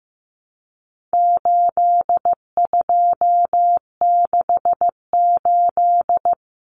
Morse code, heard as a keyed tone, about 15 wpm, 700 Hz. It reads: 8268
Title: Numbers - 4-Digit 15wpm